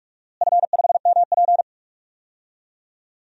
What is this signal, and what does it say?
Morse code, heard as a keyed tone, about 46 wpm, 700 Hz. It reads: F5MP
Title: DXCC-100 Call Signs - 4 Characters - Repeated 2x - 45wpm